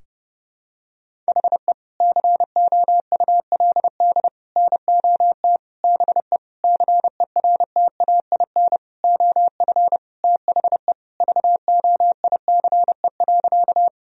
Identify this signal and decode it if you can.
Morse code, heard as a keyed tone, about 32 wpm, 700 Hz. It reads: HE COULD NOT BE CERTAIN OF THE VOICE.
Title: Sentences from Top 1000 Words - Review Part 2 - 30wpm